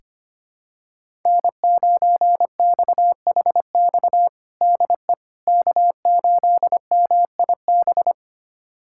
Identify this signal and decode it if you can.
Morse code, heard as a keyed tone, about 25 wpm, 700 Hz. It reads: N9XHX DE K8MIB
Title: USA Callsign DE Callsign - 5 Characters - 25wpm